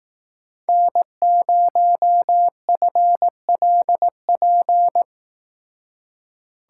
Morse code, heard as a keyed tone, about 18 wpm, 700 Hz. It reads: N0FLP
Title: Speed Racing - USA 1x3 Call Signs 15wpm